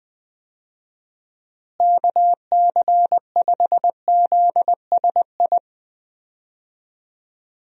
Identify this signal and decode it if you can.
Morse code, heard as a keyed tone, about 20 wpm, 700 Hz. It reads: KC5ZSI
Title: USA Call Signs - 6 Characters - Repeated 2x - 20wpm